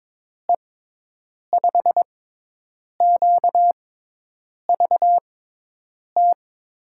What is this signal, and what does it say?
Morse code, heard as a keyed tone, about 22 wpm, spaced as for 8 wpm, 700 Hz. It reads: E5QVT
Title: USA Call Signs - 6 Characters - 6x Character Spacing - 22wpm